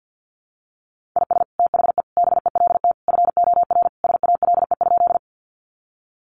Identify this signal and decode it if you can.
Morse code, heard as a keyed tone, about 50 wpm, 700 Hz. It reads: IS THE BELT FOR SALE?
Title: Sentences from Top 1400 Words 50wpm